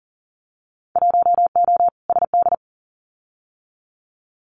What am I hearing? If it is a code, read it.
Morse code, heard as a keyed tone, about 38 wpm, 700 Hz. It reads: 1O SD
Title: Winter Field Day Exchanges Encore 40wpm